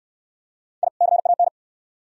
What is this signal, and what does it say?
Morse code, heard as a keyed tone, about 67 wpm, 700 Hz. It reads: I CAN